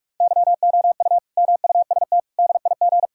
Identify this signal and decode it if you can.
Morse code, heard as a keyed tone, about 41 wpm, 700 Hz. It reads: YOU MUST DIG
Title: Sentences from Top 1600 Words 45wpm